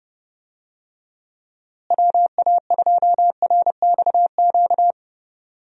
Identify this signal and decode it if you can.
Morse code, heard as a keyed tone, about 32 wpm, 700 Hz. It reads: WA2RXQ
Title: USA Call Signs - 6 Characters - Repeated 2x - 30wpm